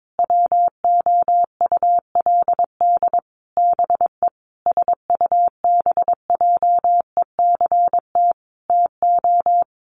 Morse code, heard as a keyed tone, about 22 wpm, 700 Hz. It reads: WOULD BE SUBJECT TO
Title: Sentences from Top 1300 Words 22wpm